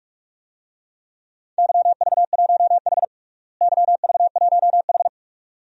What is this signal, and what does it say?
Morse code, heard as a keyed tone, about 45 wpm, 700 Hz. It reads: YV1H YV1H
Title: DXCC-100 Call Signs - 4 Characters - Repeated 2x - 45wpm